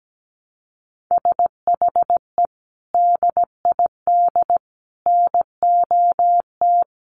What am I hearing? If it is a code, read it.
Morse code, heard as a keyed tone, about 17 wpm, 700 Hz. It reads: SHE DID NOT
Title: Sentences from Top 1100 Words 17wpm